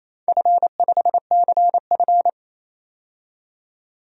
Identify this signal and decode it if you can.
Morse code, heard as a keyed tone, about 27 wpm, 700 Hz. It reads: F5CF
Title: DXCC-100 Call Signs - 4 Characters - 28wpm